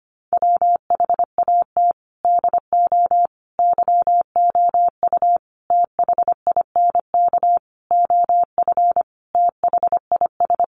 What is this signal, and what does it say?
Morse code, heard as a keyed tone, about 25 wpm, 700 Hz. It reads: WHAT DO YOU THINK OF THIS
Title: Sentences from Top 1200 Words 25wpm